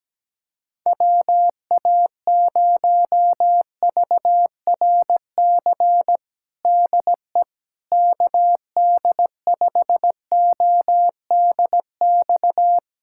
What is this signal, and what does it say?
Morse code, heard as a keyed tone, about 17 wpm, 700 Hz. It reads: WA0VRC DE KD5ODX
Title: USA Callsign DE Callsign - 6 Characters - 17wpm